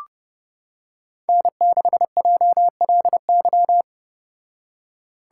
Morse code, heard as a keyed tone, about 31 wpm, 700 Hz. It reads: N6JLY